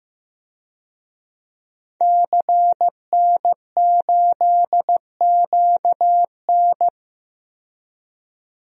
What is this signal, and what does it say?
Morse code, heard as a keyed tone, about 15 wpm, 700 Hz. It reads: CN8QN
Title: DXCC-100 Call Signs - 5 Characters - 15wpm